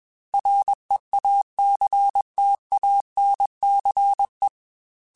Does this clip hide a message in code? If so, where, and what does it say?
Yes: Morse code from 0.3 to 4.5 s, heard as a keyed tone, about 21 wpm, 800 Hz. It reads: REACTANCE